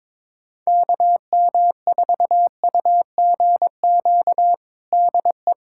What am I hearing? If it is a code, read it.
Morse code, heard as a keyed tone, about 22 wpm, 700 Hz. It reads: KM4UGQ DE
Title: USA Callsign DE Callsign 22wpm